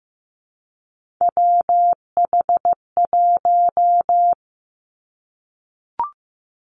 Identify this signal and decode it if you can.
Morse code, heard as a keyed tone, about 15 wpm, 700 Hz. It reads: WH1
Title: Three Challenging Letter-Number Combinations 15wpm